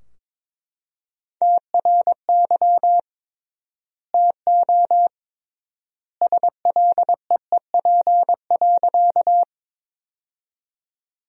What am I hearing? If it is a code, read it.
Morse code, heard as a keyed tone, about 22 wpm, 700 Hz. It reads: TRY TO SLEEP.
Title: 3x Word-Spacing - Sentences from Top 500 Words - 22wpm